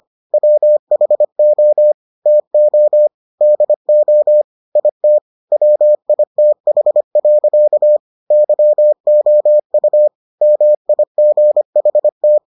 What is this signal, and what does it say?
Morse code, heard as a keyed tone, about 25 wpm, 600 Hz. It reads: WHO TO DO IT WITH. YOU MIGHT